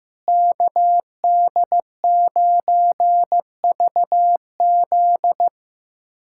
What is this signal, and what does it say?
Morse code, heard as a keyed tone, about 15 wpm, 700 Hz. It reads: KD9VZ
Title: USA Callsign DE Callsign - 5 Characters - 15wpm